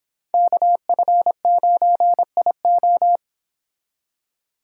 Morse code, heard as a keyed tone, about 29 wpm, 700 Hz. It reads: KF9IO